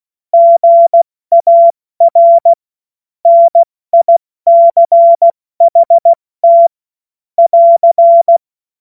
Morse code, heard as a keyed tone, about 16 wpm, 680 Hz. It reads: GAR NICHT <AR>
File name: Morsekurs / Witze / Witz 18. April 70 F80